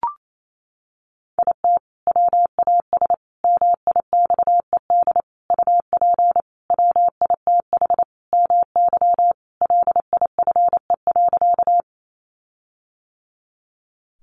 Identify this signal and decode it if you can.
Morse code, heard as a keyed tone, about 28 wpm, 700 Hz. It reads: IT WAS MIXED UP WITH MY LIFE.
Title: Sentences from Top 1500 Words 28wpm